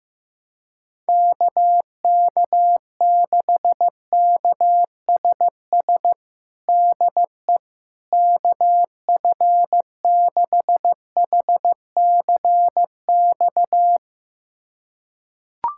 Morse code, heard as a keyed tone, about 15 wpm, 700 Hz. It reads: KK6KSS DE KF6HCX
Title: USA Callsign DE Callsign - 6 Characters - 15wpm